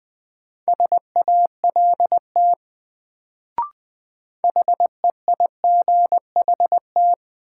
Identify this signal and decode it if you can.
Morse code, heard as a keyed tone, about 20 wpm, 700 Hz. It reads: SALT HEIGHT
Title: Sets of 5 Words - New Word Drill - 20wpm